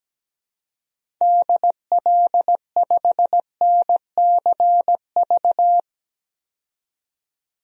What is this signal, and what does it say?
Morse code, heard as a keyed tone, about 17 wpm, 700 Hz. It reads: DL5NCV
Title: DXCC-100 Call Signs - 6 Characters - 17wpm